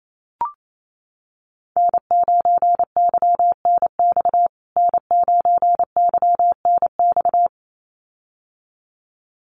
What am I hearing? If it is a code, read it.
Morse code, heard as a keyed tone, about 28 wpm, 700 Hz. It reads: N9YNX N9YNX
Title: USA Call Signs Repeated 2x 28wpm